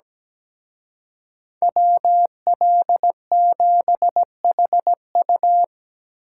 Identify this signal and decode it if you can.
Morse code, heard as a keyed tone, about 20 wpm, 700 Hz. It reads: WL7HU
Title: Speed Racing - USA Call Signs 17wpm